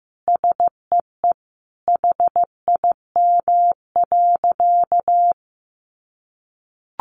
Morse code, heard as a keyed tone, about 15 wpm, 700 Hz. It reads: SEE HIM.
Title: Sentences from Top 600 Words 15wpm